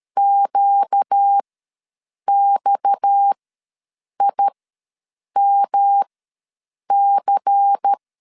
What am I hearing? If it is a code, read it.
Morse code, heard as a keyed tone, about 13 wpm, spaced as for 7 wpm, 800 Hz. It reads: QXIMC
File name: LEZIONI / 6 PKX0 / 6 PKX ws 12wpm 3106